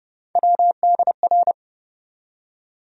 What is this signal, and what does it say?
Morse code, heard as a keyed tone, about 30 wpm, 700 Hz. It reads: WDR
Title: USA Call Sign Postfixes - 3 Letters 30wpm